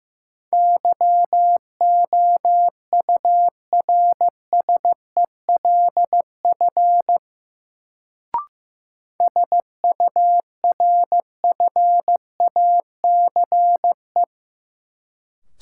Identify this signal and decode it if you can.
Morse code, heard as a keyed tone, about 15 wpm, 700 Hz. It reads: YOURSELF SURFACE
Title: Sentences from Top 600 Words - New Word Drill - 15wpm